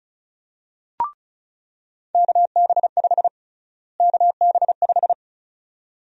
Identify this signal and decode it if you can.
Morse code, heard as a keyed tone, about 34 wpm, 700 Hz. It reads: KB5 KB5
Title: USA Call Sign Prefixes - Repeated 2x - 35wpm